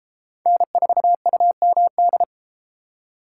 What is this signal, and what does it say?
Morse code, heard as a keyed tone, about 32 wpm, 700 Hz. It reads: N4UMD